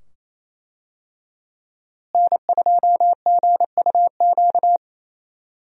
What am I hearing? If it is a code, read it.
Morse code, heard as a keyed tone, about 28 wpm, 700 Hz. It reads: N2GUQ
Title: USA Call Signs - 5 Characters - Repeated 2x - 28wpm